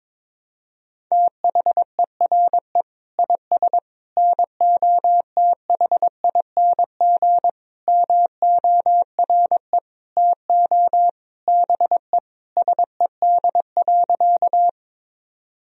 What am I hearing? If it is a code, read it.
Morse code, heard as a keyed tone, about 22 wpm, 700 Hz. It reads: THERE IS NOTHING MORE TO BE SED.
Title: Sentences from Top 500 Words and Abbreviations 22wpm